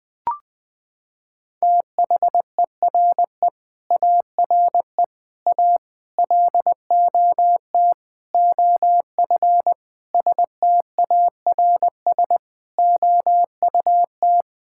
Morse code, heard as a keyed tone, about 20 wpm, 700 Hz. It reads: THERE ARE A LOT OF STARS OUT
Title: Sentences with 5-Word Phrases - 20wpm